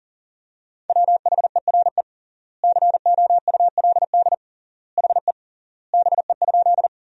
Morse code, heard as a keyed tone, about 40 wpm, 700 Hz. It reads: WHERE COULD HE BE?
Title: 3x Word-Spacing - Sentences from Top 200 Words - 40wpm